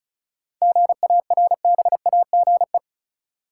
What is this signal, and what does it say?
Morse code, heard as a keyed tone, about 34 wpm, 700 Hz. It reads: GARBAGE
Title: Seven Letter Words - 35wpm